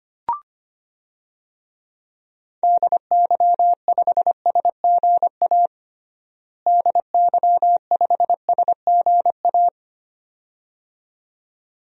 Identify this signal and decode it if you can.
Morse code, heard as a keyed tone, about 25 wpm, 700 Hz. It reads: DY5SGA DY5SGA
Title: DXCC-100 Call Signs - 6 Characters - Repeated 2x - 25wpm